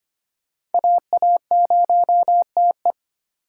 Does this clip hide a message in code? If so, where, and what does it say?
Yes: Morse code from 0.7 to 2.9 s, heard as a keyed tone, about 32 wpm, 700 Hz. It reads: AA0TE